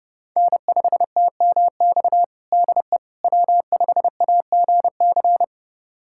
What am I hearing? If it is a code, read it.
Morse code, heard as a keyed tone, about 30 wpm, 700 Hz. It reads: N5TMX DE W5AGC
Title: USA Callsign DE Callsign - 5 Characters - 30wpm